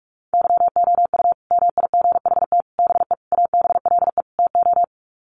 Morse code, heard as a keyed tone, about 44 wpm, 700 Hz. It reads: YOU MIGHT BE ABLE TO
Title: Sets of 5 Words Encore2 45wpm